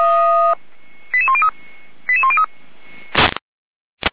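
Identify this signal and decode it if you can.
Five-tone selective call (ZVEI), heard as quick runs of five tones, tones of about 70 ms. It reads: 88103 88103
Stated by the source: ZVEI five-tone selective call received on SDR, 455 channel, fire brigade Feuerwehr Orlamünde.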